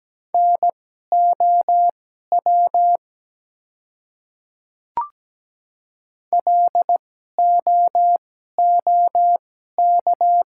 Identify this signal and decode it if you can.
Morse code, heard as a keyed tone, about 17 wpm, spaced as for 12 wpm, 700 Hz. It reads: NOW LOOK
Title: 2x Character Spacing - Top 100 Words - 17wpm